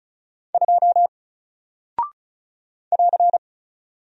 Morse code, heard as a keyed tone, about 36 wpm, 700 Hz. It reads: 2 <AR>
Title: Single Character (Letter:Number:Punctuation:Prosign) Alt 35wpm